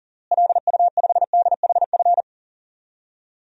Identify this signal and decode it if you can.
Morse code, heard as a keyed tone, about 40 wpm, 700 Hz. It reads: LU5DHF
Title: DXCC-100 Call Signs - 6 Characters - Repeated 3x - 40wpm